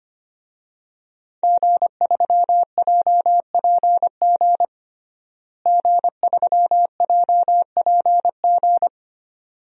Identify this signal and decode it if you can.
Morse code, heard as a keyed tone, about 25 wpm, 700 Hz. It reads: G3JPG G3JPG
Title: DXCC-100 Call Signs - 5 Characters - Repeated 2x - 25wpm